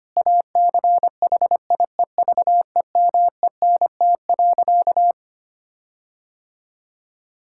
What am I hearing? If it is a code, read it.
Morse code, heard as a keyed tone, about 25 wpm, 700 Hz. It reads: ACHIEVEMENT.
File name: practice / 25wpm / Difficult Sentences with Hints - 25wpm